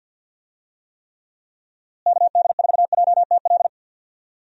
Morse code, heard as a keyed tone, about 46 wpm, 700 Hz. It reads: KD4JTL
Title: USA Call Signs - 6 Characters - Repeated 2x - 50wpm